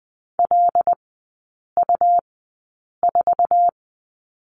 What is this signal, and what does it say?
Morse code, heard as a keyed tone, about 20 wpm, spaced as for 8 wpm, 690 Hz. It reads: LU4
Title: 166 Lesson 27e - Number 7 - Three Characters 20wpm